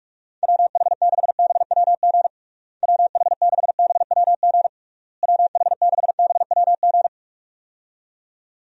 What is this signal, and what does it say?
Morse code, heard as a keyed tone, about 45 wpm, 700 Hz. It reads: WH6BWG WH6BWG WH6BWG
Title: USA Call Signs - 6 Characters - Repeated 3x - 45wpm